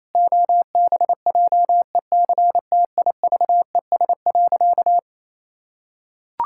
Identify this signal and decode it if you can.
Morse code, heard as a keyed tone, about 28 wpm, 700 Hz. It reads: OBJECTIVES.
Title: Pangram Challenge v2 - 28wpm